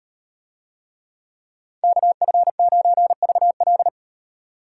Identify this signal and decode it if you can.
Morse code, heard as a keyed tone, about 38 wpm, 700 Hz. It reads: KF9VL